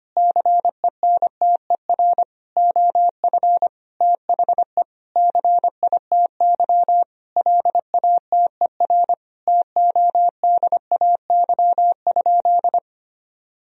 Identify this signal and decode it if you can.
Morse code, heard as a keyed tone, about 25 wpm, 700 Hz. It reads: CENTER OF THE CITY LATER TODAY?
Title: Sentences from Top 1000 Words - Review Part 2 - 25wpm